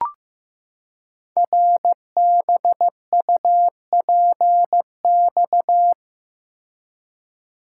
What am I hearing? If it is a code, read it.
Morse code, heard as a keyed tone, about 15 wpm, 700 Hz. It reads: RBUPX
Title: Code Groups - 5 Letters - 15wpm